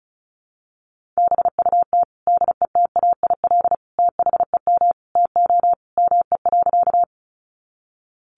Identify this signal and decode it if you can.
Morse code, heard as a keyed tone, about 34 wpm, 700 Hz. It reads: BUT DETAIL THEM TO ME.
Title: Sentences from Top 1500 Words 35wpm